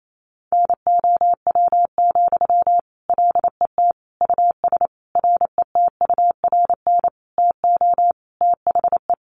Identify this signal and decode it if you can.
Morse code, heard as a keyed tone, about 28 wpm, 700 Hz. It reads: NOW, LET US RETURN TO THE
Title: Sentences from Top 800 Words 28wpm